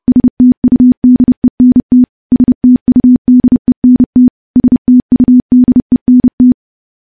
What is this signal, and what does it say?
Morse code, heard as a keyed tone, about 30 wpm, 250 Hz. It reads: STUDENT STUDENT STUDENT
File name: morse / morse-training-30WPM